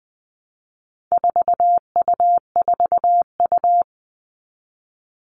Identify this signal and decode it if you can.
Morse code, heard as a keyed tone, about 20 wpm, 700 Hz. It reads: 4U4U
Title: All Cut Numbers - 4-Digit 20wpm